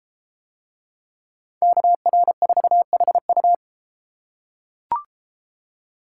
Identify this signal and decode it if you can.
Morse code, heard as a keyed tone, about 32 wpm, 700 Hz. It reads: KR4HU